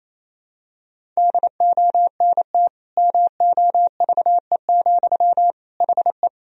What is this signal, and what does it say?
Morse code, heard as a keyed tone, about 28 wpm, 700 Hz. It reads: DONT MOVE, HE
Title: Sentences from Top 300 Words and Abbreviations 28wpm